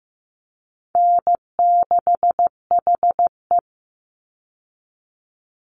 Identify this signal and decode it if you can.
Morse code, heard as a keyed tone, about 15 wpm, 700 Hz. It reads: N6HE
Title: Easy Call Signs Repeated 3x - 15wpm